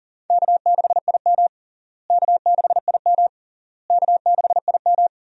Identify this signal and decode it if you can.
Morse code, heard as a keyed tone, about 37 wpm, 700 Hz. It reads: K6IM K6IM K6IM
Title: USA Call Signs - 4 Characters - Repeated 3x - 40wpm